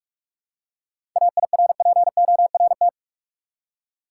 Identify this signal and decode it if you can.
Morse code, heard as a keyed tone, about 46 wpm, 700 Hz. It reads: AIRPORT
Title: Seven Letter Words - 45wpm